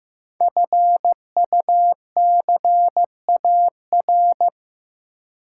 Morse code, heard as a keyed tone, about 15 wpm, 700 Hz. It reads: FUCAR
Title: Code Groups - 5 Letter:Numbers - 15wpm